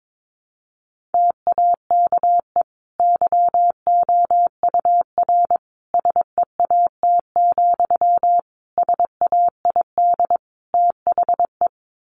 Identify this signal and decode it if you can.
Morse code, heard as a keyed tone, about 22 wpm, 700 Hz. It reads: TAKE YOUR SEAT, SAID THE
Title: Sentences from Top 1500 Words 22wpm